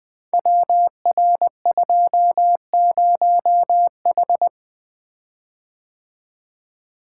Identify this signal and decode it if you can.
Morse code, heard as a keyed tone, about 20 wpm, 700 Hz. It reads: WR20H
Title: Code Groups - 5 Letter:Numbers - 20wpm